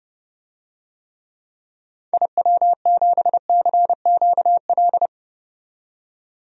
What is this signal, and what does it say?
Morse code, heard as a keyed tone, about 30 wpm, 700 Hz. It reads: IW7CQL
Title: DXCC-100 Call Signs - 6 Characters - 30wpm